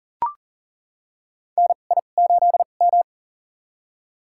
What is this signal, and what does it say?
Morse code, heard as a keyed tone, about 38 wpm, spaced as for 28 wpm, 700 Hz. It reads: NI8M
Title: USA Call Signs - 4 Characters - 2x Character Spacing - 40wpm